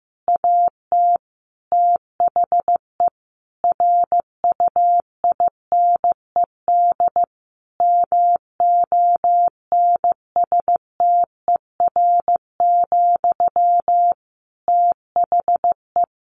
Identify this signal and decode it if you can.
Morse code, heard as a keyed tone, about 14 wpm, 700 Hz. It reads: AT THE RUINED MONSTER, THE
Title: A Sound Of Thunder 15wpm